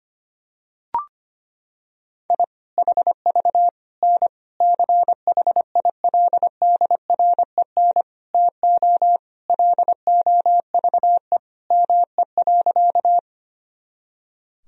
Morse code, heard as a keyed tone, about 25 wpm, 700 Hz. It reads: I HV N CHILDREN TO LOVE ME.
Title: Sentences from Top 400 Words and Abbreviations 25wpm